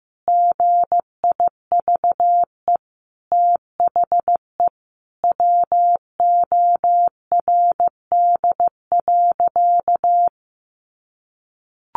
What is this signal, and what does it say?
Morse code, heard as a keyed tone, about 15 wpm, 700 Hz. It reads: GIVE THE WORD.
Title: A Sound Of Thunder 15wpm